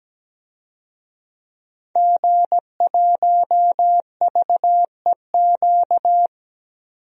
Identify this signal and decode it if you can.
Morse code, heard as a keyed tone, about 17 wpm, 700 Hz. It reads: G1VEQ
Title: DXCC-100 Call Signs - 5 Characters - 17wpm